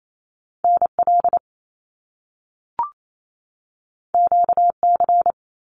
Morse code, heard as a keyed tone, about 28 wpm, 700 Hz. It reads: NL QC
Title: Canadian Provinces and Territories 28wpm